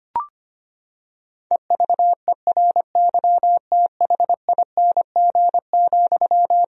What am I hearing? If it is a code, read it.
Morse code, heard as a keyed tone, about 25 wpm, 700 Hz. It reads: EVERYTHING,
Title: Sentences from Top 1000 Words - Review Part 1 - 25wpm